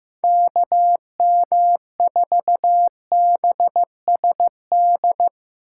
Morse code, heard as a keyed tone, about 16 wpm, 700 Hz. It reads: KM4BSD
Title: Speed Racing - USA 2x3 Call Signs 15wpm